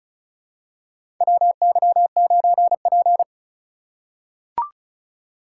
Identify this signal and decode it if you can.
Morse code, heard as a keyed tone, about 36 wpm, 700 Hz. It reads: WY9P
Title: USA Call Signs - 4 Characters - 3x Character Spacing - 35wpm